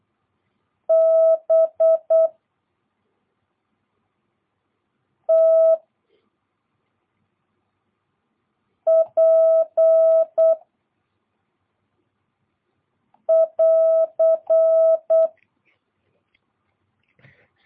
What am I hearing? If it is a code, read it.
Morse code, heard as a keyed tone, about 8 wpm, 650 Hz. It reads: B T P <AR>